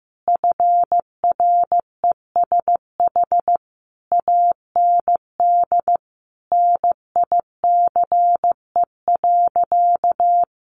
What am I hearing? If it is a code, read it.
Morse code, heard as a keyed tone, about 14 wpm, 700 Hz. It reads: FRESH AND NICE.
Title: Sentences from Top 1200 Words 15wpm